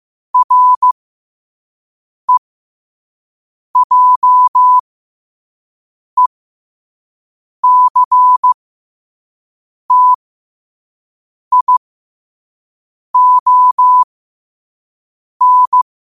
Morse code, heard as a keyed tone, about 15 wpm, spaced as for 5 wpm, 1000 Hz. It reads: REJECTION